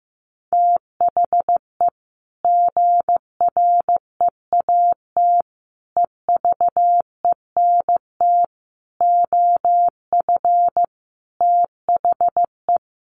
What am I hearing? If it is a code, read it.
Morse code, heard as a keyed tone, about 15 wpm, 700 Hz. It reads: THE GREAT EVENT OF THE
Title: Sentences from Top 1500 Words 15wpm